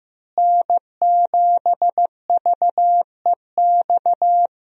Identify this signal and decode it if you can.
Morse code, heard as a keyed tone, about 15 wpm, 700 Hz. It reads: N7VEX
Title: USA Call Signs - 5 Characters - Repeated 2x - 15wpm